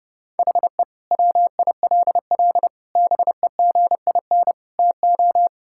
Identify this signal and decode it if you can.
Morse code, heard as a keyed tone, about 30 wpm, 700 Hz. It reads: HE WILL BEGIN TO
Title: Sentences from Top 1600 Words 30wpm